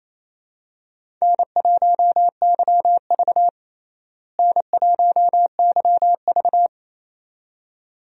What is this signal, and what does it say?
Morse code, heard as a keyed tone, about 28 wpm, 700 Hz. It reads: N1YV N1YV
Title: USA Call Signs - 4 Characters - Repeated 2x - 28wpm